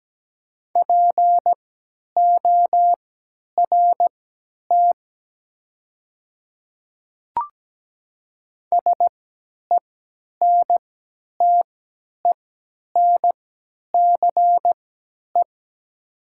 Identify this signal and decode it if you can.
Morse code, heard as a keyed tone, about 17 wpm, spaced as for 10 wpm, 690 Hz. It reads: PORT SENTENCE
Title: 3x Character Spacing - Top 300 Words - 17wpm